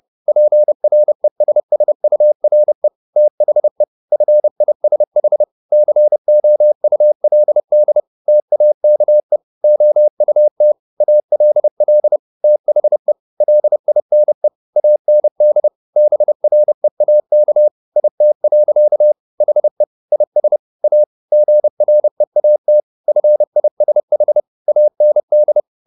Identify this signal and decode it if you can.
Morse code, heard as a keyed tone, about 30 wpm, 600 Hz. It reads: PRESSURE THE FISH COULD TAKE OUT ALL THE LINE AND BREAK IT. HE IS A GREAT FISH AND
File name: the old man and the sea / the old man and the sea 1 30wpm